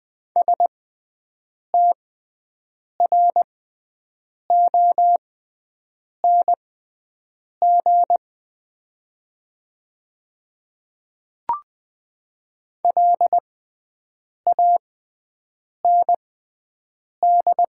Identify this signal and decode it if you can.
Morse code, heard as a keyed tone, about 20 wpm, spaced as for 7 wpm, 700 Hz. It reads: STRONG LAND
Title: 6x Character Spacing - Top 500 Words - 20wpm